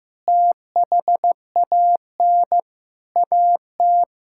Morse code, heard as a keyed tone, about 15 wpm, 700 Hz. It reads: THAN AT
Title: Sentences from Top 1000 Words - Review Part 5 - 15wpm